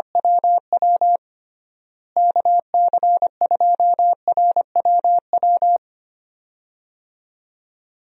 Morse code, heard as a keyed tone, about 25 wpm, 700 Hz. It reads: WW KC2RWW
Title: USA Call Signs - 6 Characters - Repeated 2x - 25wpm